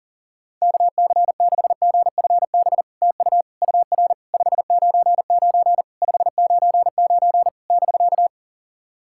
Morse code, heard as a keyed tone, about 38 wpm, 700 Hz. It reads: KC6GFB TU UR 599 599 <BK>
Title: SOTA QSO Practice 40wpm